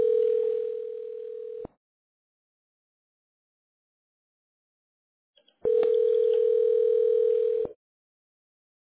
Ringback tone (North America), 440 + 480 Hz.